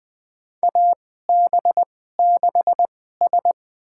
Morse code, heard as a keyed tone, about 21 wpm, spaced as for 15 wpm, 700 Hz. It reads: AB6S
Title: USA Call Signs - 4 Characters - 2x Character Spacing - 20wpm